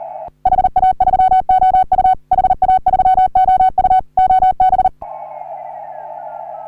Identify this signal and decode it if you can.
Morse code, heard as a keyed tone, about 39 wpm, 750 Hz. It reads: HA3OU HA3OU OB
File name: CW / HA3OU RHOBCW REC